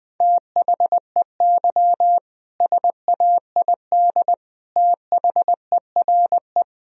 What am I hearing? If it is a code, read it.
Morse code, heard as a keyed tone, about 20 wpm, 700 Hz. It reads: THEY SAID THERE
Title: Sentences from Top 1000 Words - Review Part 2 - 20wpm